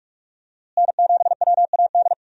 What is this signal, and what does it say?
Morse code, heard as a keyed tone, about 41 wpm, 700 Hz. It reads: N7WAD
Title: USA Call Signs - 5 Characters - Repeated 2x - 45wpm